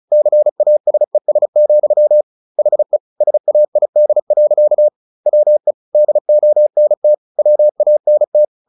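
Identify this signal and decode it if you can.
Morse code, heard as a keyed tone, about 34 wpm, 600 Hz. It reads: CASES, HE SAID. WE DONT WANT
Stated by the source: Morse practice at 35 wpm from CW med Gnister, sending Ken Follett's novel Never in English.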